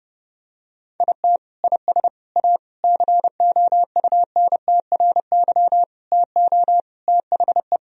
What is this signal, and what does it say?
Morse code, heard as a keyed tone, about 30 wpm, 700 Hz. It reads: IT IS A COUNTRY TO THE
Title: Sentences from Top 1000 Words - Review Part 1 - 30wpm